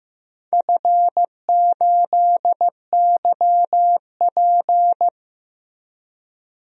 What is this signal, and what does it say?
Morse code, heard as a keyed tone, about 15 wpm, 700 Hz. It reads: F8YP
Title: DXCC-100 Call Signs - 4 Characters - Repeated 2x - 15wpm